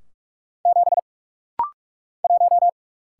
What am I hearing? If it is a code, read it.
Morse code, heard as a keyed tone, about 48 wpm, 690 Hz. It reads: 7 1
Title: Single Letter-Number - Rapid-Fire 45wpm